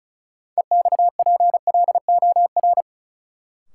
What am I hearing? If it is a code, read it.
Morse code, heard as a keyed tone, about 36 wpm, 700 Hz. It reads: EXPLOR
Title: Sentences from Top 1100 Words - New Word Drill Alt - 35wpm